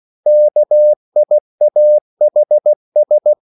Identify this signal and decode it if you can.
Morse code, heard as a keyed tone, about 16 wpm, 600 Hz. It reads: KIAHS